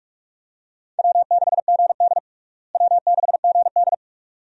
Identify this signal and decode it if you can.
Morse code, heard as a keyed tone, about 45 wpm, 700 Hz. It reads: W6GD W6GD
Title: USA Call Signs - 4 Characters - Repeated 2x - 45wpm